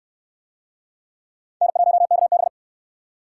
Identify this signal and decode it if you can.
Morse code, heard as a keyed tone, about 68 wpm, 700 Hz. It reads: N1KB